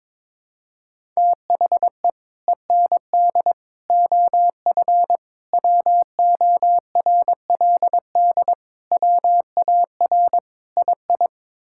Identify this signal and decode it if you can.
Morse code, heard as a keyed tone, about 22 wpm, 700 Hz. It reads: THE END OF WORLD WAR II
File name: practice / 22wpm / Sets of 6 Words - 22wpm